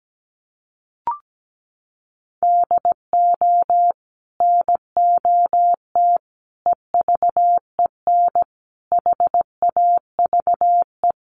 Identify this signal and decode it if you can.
Morse code, heard as a keyed tone, about 17 wpm, 700 Hz. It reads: DO NOT EVEN HAVE
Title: Sets of 5 Words Encore2 17wpm